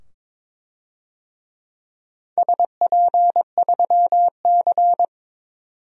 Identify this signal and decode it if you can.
Morse code, heard as a keyed tone, about 22 wpm, 700 Hz. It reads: SP3C
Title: DXCC-100 Call Signs - 4 Characters - 22wpm